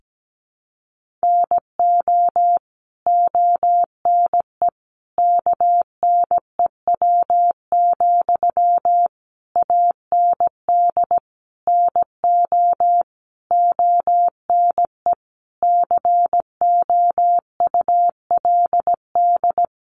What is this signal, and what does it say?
Morse code, heard as a keyed tone, about 17 wpm, 700 Hz. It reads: NO ONE KNEW, AND NO ONE COULD
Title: Sentences from Top 1400 Words 17wpm